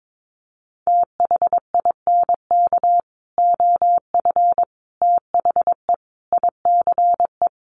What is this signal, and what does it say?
Morse code, heard as a keyed tone, about 22 wpm, 700 Hz. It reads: THINK OF THE ICE
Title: Sentences from Top 1500 Words 22wpm